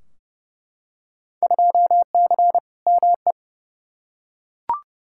Morse code, heard as a keyed tone, about 30 wpm, 700 Hz. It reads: 2C ME
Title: Field Day Exchanges Encore - 30wpm